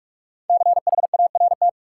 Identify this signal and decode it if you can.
Morse code, heard as a keyed tone, about 41 wpm, 700 Hz. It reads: CHART
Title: Sentences from Top 1300 Words - New Word Drill Alt - 45wpm